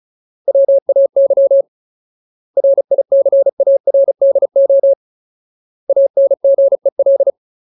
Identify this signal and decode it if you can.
Morse code, heard as a keyed tone, about 34 wpm, 550 Hz. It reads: WAY RICARDO ANGEL
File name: Ultimate Word List / 35WPM Ultimate